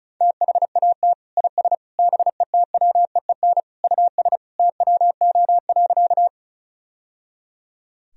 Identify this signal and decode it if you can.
Morse code, heard as a keyed tone, about 36 wpm, 700 Hz. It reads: THAT IS BETWEEN US TWO.
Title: Sentences from Top 1000 Words - Review Part 1 - 35wpm